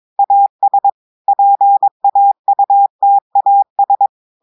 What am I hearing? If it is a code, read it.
Morse code, heard as a keyed tone, about 22 wpm, 800 Hz. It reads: AS PAUTAS
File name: audios / portugais / portugais texte en clair 2 vitesse 22 mots minutes